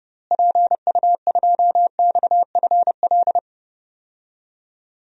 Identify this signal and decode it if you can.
Morse code, heard as a keyed tone, about 30 wpm, 700 Hz. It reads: PU2XFL
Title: DXCC-100 Call Signs - 6 Characters - Repeated 2x - 30wpm